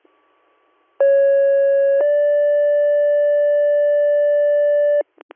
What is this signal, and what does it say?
Two-tone page, heard as a one-second tone, then a three-second tone, about 550 then 580 Hz. It is heard from 1.0 to 5.0 s.